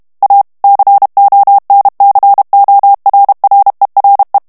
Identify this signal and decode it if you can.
Morse code, heard as a keyed tone, about 32 wpm, 800 Hz. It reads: A CONCORRERE
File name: texte clair 2 vitesse 30 mots minute